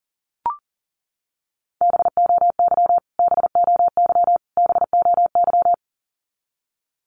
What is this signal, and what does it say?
Morse code, heard as a keyed tone, about 40 wpm, 700 Hz. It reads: BOY BOY BOY
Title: Top 200 Words Repeated 40wpm